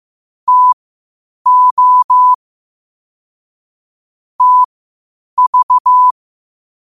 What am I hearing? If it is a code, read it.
Morse code, heard as a keyed tone, about 15 wpm, spaced as for 8 wpm, 1000 Hz. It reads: TO TV
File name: qotdfiles / 8wpm